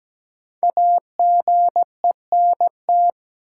Morse code, heard as a keyed tone, about 17 wpm, 700 Hz. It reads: AGENT
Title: Sentences from Top 1600 Words - New Word Drill Alt - 17wpm